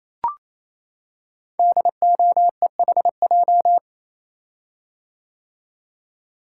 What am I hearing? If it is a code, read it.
Morse code, heard as a keyed tone, about 27 wpm, 700 Hz. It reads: DOEHJ
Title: Code Groups - 5 Letters - 28wpm